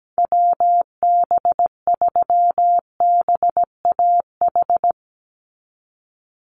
Morse code, heard as a keyed tone, about 17 wpm, 700 Hz. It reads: WB3BAH
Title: USA Call Signs 17wpm